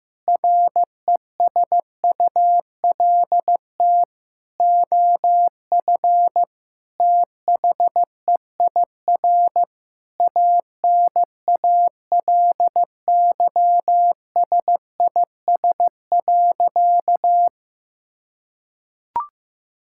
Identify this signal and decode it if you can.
Morse code, heard as a keyed tone, about 15 wpm, 700 Hz. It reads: RESULT OF THEIR ANALYSIS.
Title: Sentences from Top 1000 Words - Review Part 5 - 15wpm